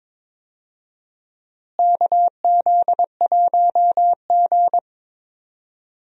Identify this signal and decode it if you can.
Morse code, heard as a keyed tone, about 21 wpm, 700 Hz. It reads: KZ1G
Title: USA Call Signs - 4 Characters - Repeated 3x - 22wpm